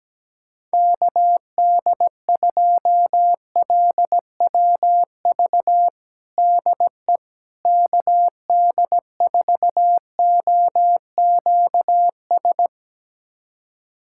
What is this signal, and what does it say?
Morse code, heard as a keyed tone, about 17 wpm, 700 Hz. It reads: KD2LWV DE KD4OQS
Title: USA Callsign DE Callsign - 6 Characters - 17wpm